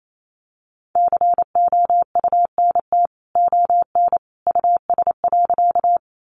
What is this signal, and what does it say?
Morse code, heard as a keyed tone, about 28 wpm, 700 Hz. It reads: COUNT ON US.
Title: Sentences from Top 1500 Words 28wpm